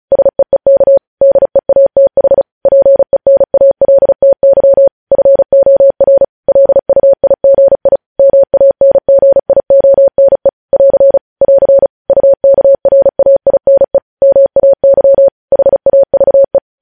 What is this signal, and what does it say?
Morse code, heard as a keyed tone, about 36 wpm, 550 Hz. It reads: SEEK DEATH PENALTY FOR LUIGI MANGIONE <AR> <AR> UKRAINE MAY HAVE
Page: News headlines in Morse code at 35 wpm